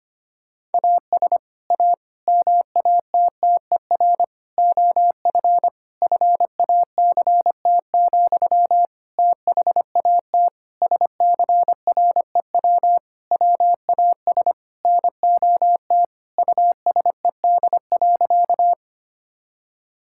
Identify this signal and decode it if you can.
Morse code, heard as a keyed tone, about 25 wpm, 700 Hz. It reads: AS A MATTER OF FACT, THAT SCREW WAS NOT USED.
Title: Sentences from Top 1600 Words 25wpm